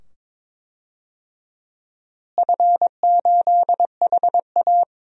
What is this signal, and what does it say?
Morse code, heard as a keyed tone, about 22 wpm, 700 Hz. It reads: F8HA
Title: DXCC-100 Call Signs - 4 Characters - Repeated 2x - 22wpm